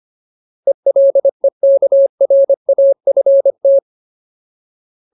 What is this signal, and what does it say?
Morse code, heard as a keyed tone, about 25 wpm, 550 Hz. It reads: ELEKRAFT
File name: Ultimate Word List / 25WPM Ultimate